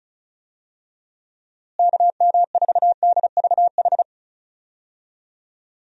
Morse code, heard as a keyed tone, about 34 wpm, 700 Hz. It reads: KM4DVH
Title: USA Call Signs - 6 Characters - Repeated 3x - 35wpm